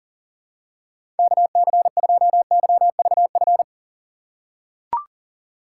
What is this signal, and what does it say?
Morse code, heard as a keyed tone, about 40 wpm, 700 Hz. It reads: KC2YVF
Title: USA Call Signs - 6 Characters - 2x Character Spacing - 40wpm